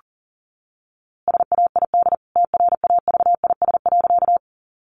Morse code, heard as a keyed tone, about 40 wpm, 700 Hz. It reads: SAID TRAVIS.
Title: A Sound Of Thunder 40wpm